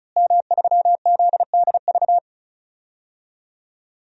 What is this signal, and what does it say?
Morse code, heard as a keyed tone, about 36 wpm, 700 Hz. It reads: M3ZDV
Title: DXCC-100 Call Signs - 5 Characters - Repeated 3x - 35wpm